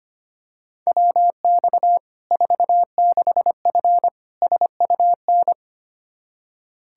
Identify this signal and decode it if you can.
Morse code, heard as a keyed tone, about 25 wpm, 700 Hz. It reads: WX 46F SUN
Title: Practice QSO - Exchange 2 - Info - 25wpm